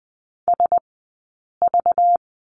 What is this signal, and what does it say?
Morse code, heard as a keyed tone, about 20 wpm, 700 Hz. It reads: S V
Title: 190 Lesson 31d - Letter v - Two Characters 20wpm